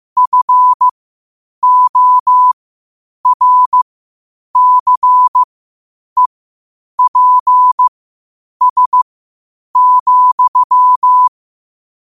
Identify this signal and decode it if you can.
Morse code, heard as a keyed tone, about 15 wpm, spaced as for 8 wpm, 1000 Hz. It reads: FORCEPS,
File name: qotdfiles / 8wpm